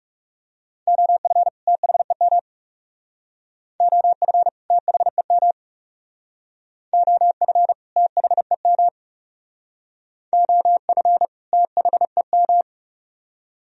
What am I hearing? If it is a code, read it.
Morse code, heard as a keyed tone, about 37 wpm, 700 Hz. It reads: OF THEM OF THEM OF THEM OF THEM
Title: Speed-Racing - Sets of 2 Words 30wpm